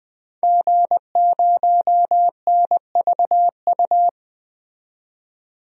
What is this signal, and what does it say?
Morse code, heard as a keyed tone, about 20 wpm, 700 Hz. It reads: G0NVU
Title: DXCC-100 Call Signs - 5 Characters - 20wpm